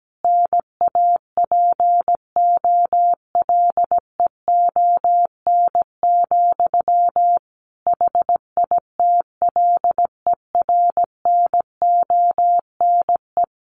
Morse code, heard as a keyed tone, about 17 wpm, 700 Hz. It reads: NAPOLEON, HITLER NONE
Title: A Sound Of Thunder 17wpm